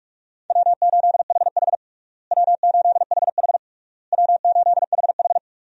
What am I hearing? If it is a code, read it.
Morse code, heard as a keyed tone, about 46 wpm, 700 Hz. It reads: W8HH W8HH W8HH
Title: USA Call Signs - 4 Characters - Repeated 3x - 45wpm